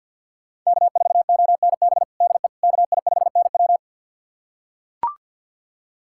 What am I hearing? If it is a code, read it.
Morse code, heard as a keyed tone, about 45 wpm, 700 Hz. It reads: K4ONB DE KI5NW
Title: USA Callsign DE Callsign 50wpm